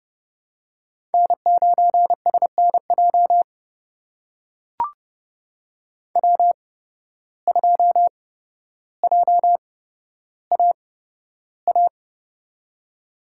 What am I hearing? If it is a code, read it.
Morse code, heard as a keyed tone, about 30 wpm, spaced as for 13 wpm, 700 Hz. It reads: N9SNJ W2JAA
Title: USA Call Signs - 8x Character Spacing - 30wpm